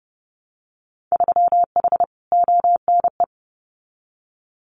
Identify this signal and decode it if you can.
Morse code, heard as a keyed tone, about 30 wpm, 700 Hz. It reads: 3H ONE
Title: Winter Field Day Exchanges Encore 30wpm